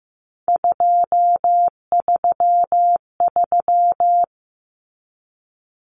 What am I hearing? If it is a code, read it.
Morse code, heard as a keyed tone, about 15 wpm, 700 Hz. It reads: 233
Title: Common Cut Numbers - 3-Digit 15wpm